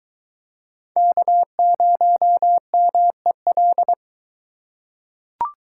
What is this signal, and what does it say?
Morse code, heard as a keyed tone, about 22 wpm, 700 Hz. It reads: K0MEL